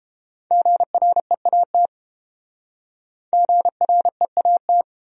Morse code, heard as a keyed tone, about 30 wpm, 700 Hz. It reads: GREAT GREAT